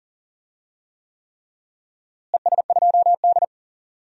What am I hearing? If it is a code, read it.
Morse code, heard as a keyed tone, about 40 wpm, 700 Hz. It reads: ES2D
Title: DXCC-100 Call Signs - 4 Characters - Repeated 3x - 40wpm